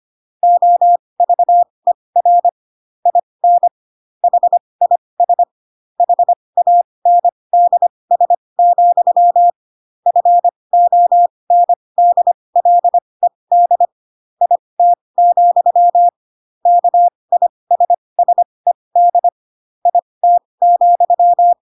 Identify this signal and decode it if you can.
Morse code, heard as a keyed tone, about 25 wpm, spaced as for 20 wpm, 700 Hz. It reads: OVER IN HIS HANDS, FONDLED IT, KISSED IT,